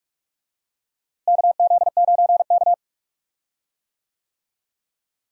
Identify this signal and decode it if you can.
Morse code, heard as a keyed tone, about 42 wpm, 700 Hz. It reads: KZ9K
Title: USA Call Signs - 4 Characters - Repeated 3x - 45wpm